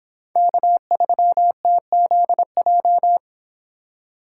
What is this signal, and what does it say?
Morse code, heard as a keyed tone, about 26 wpm, 700 Hz. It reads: K3TZJ